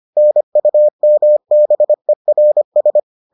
Morse code, heard as a keyed tone, about 25 wpm, 600 Hz. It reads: NUMBERS